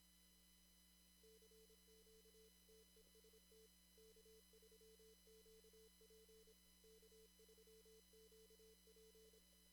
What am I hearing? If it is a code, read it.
Morse code, heard as a keyed tone, about 25 wpm, 440 Hz, with noise. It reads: CQ TEST K3QP K3QP
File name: CW / kt5j wpx cw-op-k5pi